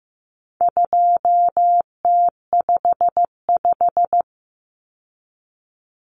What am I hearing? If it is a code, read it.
Morse code, heard as a keyed tone, about 15 wpm, 700 Hz. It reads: 2T55
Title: Common Cut Numbers - 4-Digit 15wpm